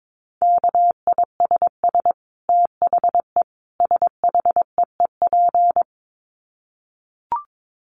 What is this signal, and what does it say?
Morse code, heard as a keyed tone, about 22 wpm, 700 Hz. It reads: KISS THE SHEEP
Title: Dit Challenge 22wpm